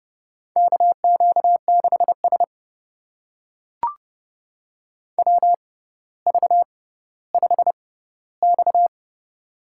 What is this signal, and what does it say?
Morse code, heard as a keyed tone, about 32 wpm, spaced as for 16 wpm, 700 Hz. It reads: KQ6S WV5X
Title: USA Call Signs - 4 Characters - 6x Character Spacing - 30wpm